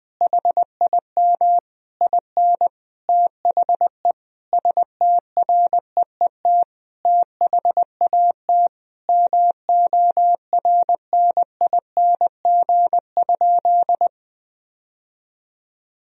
Morse code, heard as a keyed tone, about 20 wpm, 700 Hz. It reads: HIM IN THE STREET THAT MORNING?
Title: Sentences from Top 1000 Words - Review Part 3 - 20wpm